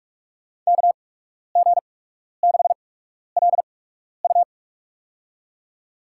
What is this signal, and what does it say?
Morse code, heard as a keyed tone, about 46 wpm, spaced as for 12 wpm, 700 Hz. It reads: KG6LU
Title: USA Call Signs - 5 Characters - 8x Character Spacing - 45wpm